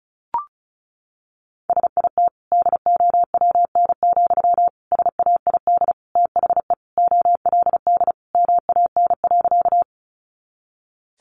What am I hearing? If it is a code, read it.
Morse code, heard as a keyed tone, about 34 wpm, 700 Hz. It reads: SIT DOWN, SAID THE OLD MAN.
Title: Sentences from Top 500 Words 35wpm